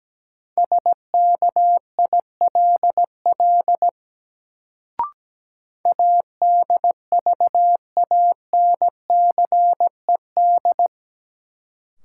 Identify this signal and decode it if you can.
Morse code, heard as a keyed tone, about 17 wpm, 700 Hz. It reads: SKILL ADVANCED
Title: Sentences from Top 1200 Words - New Word Drill Alt - 17wpm